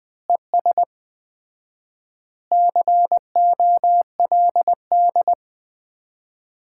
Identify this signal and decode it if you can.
Morse code, heard as a keyed tone, about 21 wpm, 700 Hz. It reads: ES COLD
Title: Common QSO Phrases - 4x Word-Spacing - 20wpm